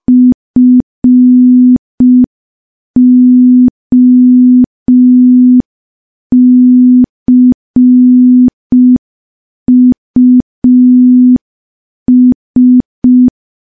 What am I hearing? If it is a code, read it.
Morse code, heard as a keyed tone, about 5 wpm, 250 Hz. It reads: FOCUS
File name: morse / morse-training-05WPM